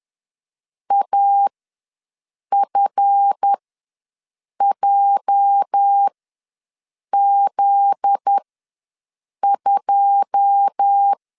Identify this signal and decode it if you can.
Morse code, heard as a keyed tone, about 10 wpm, spaced as for 6 wpm, 800 Hz. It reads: AFJZ2
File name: LEZIONI / 9 286 ws 10wpm 4032